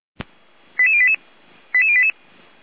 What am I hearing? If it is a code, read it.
Five-tone selective call (ZVEI), heard as quick runs of five tones, tones of about 70 ms. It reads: 88088 88088